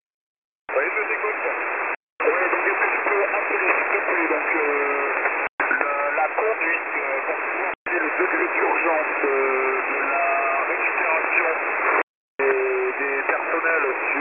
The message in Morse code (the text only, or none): none